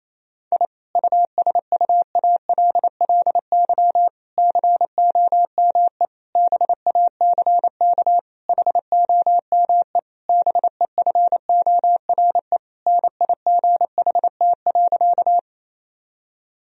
Morse code, heard as a keyed tone, about 28 wpm, 700 Hz. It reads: I USUALLY COME BACK HOME BEFORE NIGHT.
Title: Sentences from Top 1000 Words - Review Part 3 - 28wpm